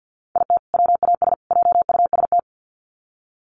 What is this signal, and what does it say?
Morse code, heard as a keyed tone, about 48 wpm, 700 Hz. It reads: IT WAS JUST
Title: Sets of 3 Words 50wpm